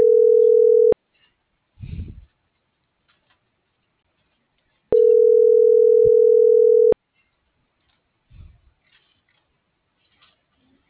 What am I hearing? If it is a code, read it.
Ringback tone (North America), 440 + 480 Hz.